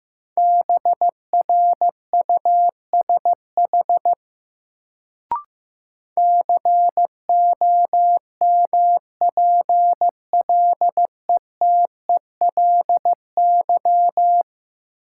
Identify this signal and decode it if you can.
Morse code, heard as a keyed tone, about 15 wpm, 700 Hz. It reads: BRUSH COMPLETELY
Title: Sentences from Top 1100 Words - New Word Drill - 15wpm